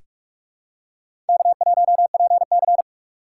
Morse code, heard as a keyed tone, about 45 wpm, 700 Hz. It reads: K1PC
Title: USA Call Signs - 6x Character Spacing - 45wpm